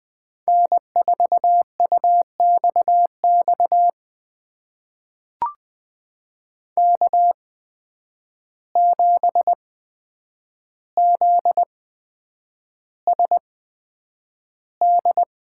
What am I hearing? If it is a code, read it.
Morse code, heard as a keyed tone, about 20 wpm, spaced as for 9 wpm, 700 Hz. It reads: N4UXX K7ZSD
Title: USA Call Signs - 5 Characters - 8x Character Spacing - 20wpm